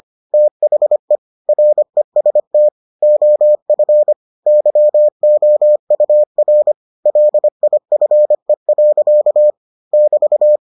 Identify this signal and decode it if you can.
Morse code, heard as a keyed tone, about 25 wpm, 600 Hz. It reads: THE REST OF YOUR LIFE. <BT>